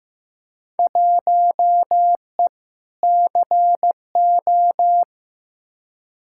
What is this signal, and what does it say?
Morse code, heard as a keyed tone, about 15 wpm, 700 Hz. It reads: 1E CO
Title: Field Day Exchanges - 15wpm